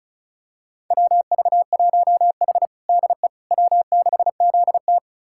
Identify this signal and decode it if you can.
Morse code, heard as a keyed tone, about 36 wpm, 700 Hz. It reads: WV1H DE W6ZT
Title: USA Callsign DE Callsign - 4 Characters - 35wpm